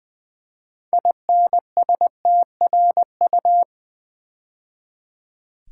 Morse code, heard as a keyed tone, about 21 wpm, 700 Hz. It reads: INSTRU
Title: Build A Word - 2 Letters At A Time - 8-Letter Words - 20wpm